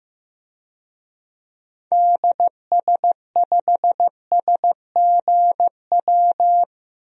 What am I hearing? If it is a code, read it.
Morse code, heard as a keyed tone, about 15 wpm, 700 Hz. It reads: DS5SGW
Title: DXCC-100 Call Signs - 6 Characters - Repeated 3x - 15wpm